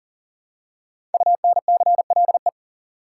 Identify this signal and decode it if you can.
Morse code, heard as a keyed tone, about 40 wpm, 700 Hz. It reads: UNCLE
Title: Five Letter Words - 40wpm